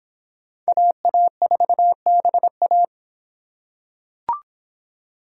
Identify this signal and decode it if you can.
Morse code, heard as a keyed tone, about 26 wpm, 700 Hz. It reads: AA4BA